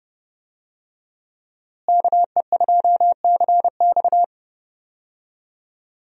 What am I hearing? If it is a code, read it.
Morse code, heard as a keyed tone, about 32 wpm, 700 Hz. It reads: KE2CX
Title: USA Call Signs - 5 Characters - Repeated 2x - 30wpm